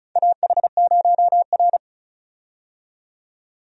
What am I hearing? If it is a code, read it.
Morse code, heard as a keyed tone, about 36 wpm, 700 Hz. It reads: AH0R
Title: USA Call Signs - 4 Characters - 35wpm